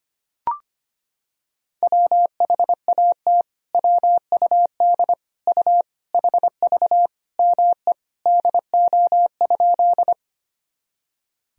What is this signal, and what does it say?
Morse code, heard as a keyed tone, about 25 wpm, 700 Hz. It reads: WHAT WUD U HV ME DO?
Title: Sentences from Top 200 Words and Abbreviations 25wpm